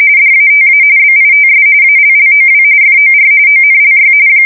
RTTY, about 45.45 baud. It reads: DATA BITS ALLOW FOR ONLY